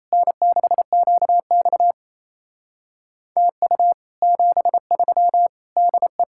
Morse code, heard as a keyed tone, about 37 wpm, 700 Hz. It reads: N6QX TU 73 DE